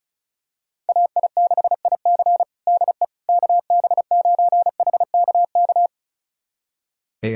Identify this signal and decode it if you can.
Morse code, heard as a keyed tone, about 36 wpm, 700 Hz. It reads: AI6IC DE KB9HKK
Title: USA Callsign DE Callsign 35wpm